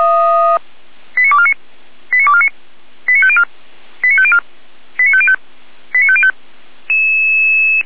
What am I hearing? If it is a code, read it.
Five-tone selective call (ZVEI), heard as quick runs of five tones, tones of about 70 ms. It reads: 89249 89249 89504 89504 89505 89505
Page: /455/SDR/ZVEI/89310/